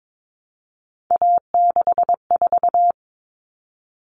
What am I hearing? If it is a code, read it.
Morse code, heard as a keyed tone, about 22 wpm, 700 Hz. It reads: A64
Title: Common Cut Numbers - 3-Digit 22wpm